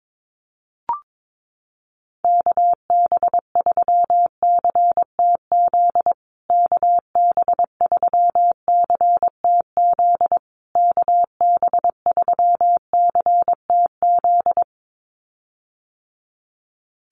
Morse code, heard as a keyed tone, about 22 wpm, 700 Hz. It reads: KB3CTZ KB3CTZ KB3CTZ
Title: USA Call Signs Repeated 3x 22wpm